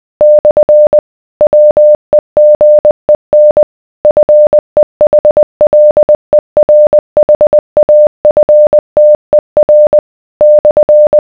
Morse code, heard as a keyed tone, about 20 wpm, 600 Hz. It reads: / WEGEN FEHLERHAFTER /